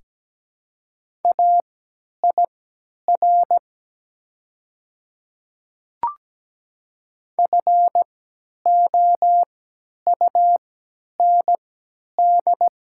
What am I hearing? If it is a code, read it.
Morse code, heard as a keyed tone, about 17 wpm, spaced as for 10 wpm, 700 Hz. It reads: AIR FOUND
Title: 3x Character Spacing - Top 300 Words - 17wpm